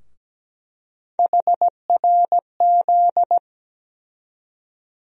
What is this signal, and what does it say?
Morse code, heard as a keyed tone, about 17 wpm, 700 Hz. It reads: HRZ
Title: USA Call Sign Postfixes - 3 Letters 17wpm